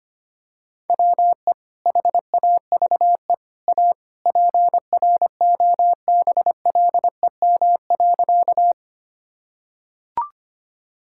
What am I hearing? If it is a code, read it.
Morse code, heard as a keyed tone, about 25 wpm, 700 Hz. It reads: WE HAVE A PROBLEM.
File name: practice / 25wpm / Sentences with 3-Word Phrases - 25wpm